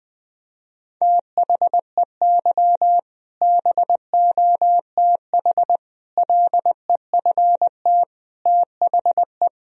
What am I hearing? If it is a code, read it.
Morse code, heard as a keyed tone, about 20 wpm, 700 Hz. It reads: THEY BOTH LEFT THE
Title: Sentences from Top 1000 Words - Review Part 1 - 20wpm